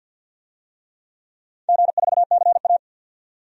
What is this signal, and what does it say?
Morse code, heard as a keyed tone, about 51 wpm, 700 Hz. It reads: G4CA
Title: DXCC-100 Call Signs - 4 Characters - 50wpm